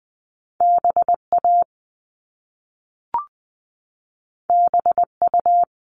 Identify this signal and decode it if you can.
Morse code, heard as a keyed tone, about 20 wpm, 700 Hz. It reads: BA BU
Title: All Cut Numbers - 2-Digit 20wpm